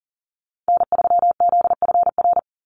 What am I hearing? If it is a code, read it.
Morse code, heard as a keyed tone, about 40 wpm, 690 Hz. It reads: N3ZFR
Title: USA Call Signs 40wpm